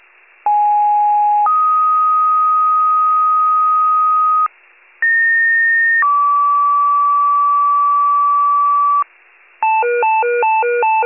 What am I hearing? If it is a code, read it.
Two-tone page, heard as a one-second tone, then a three-second tone, about 830 then 1250 Hz, about 1850 then 1150 Hz. It is heard from 0.5 to 9.0 s.